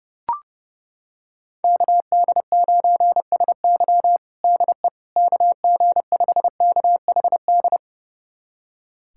Morse code, heard as a keyed tone, about 29 wpm, 700 Hz. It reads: KD9SY DE KG5KHD
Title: USA Callsign DE Callsign 30wpm